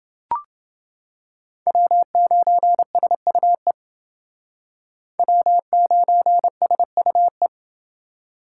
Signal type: Morse code

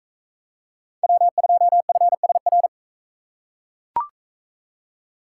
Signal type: Morse code

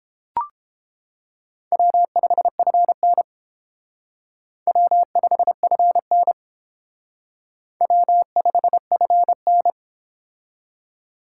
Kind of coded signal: Morse code

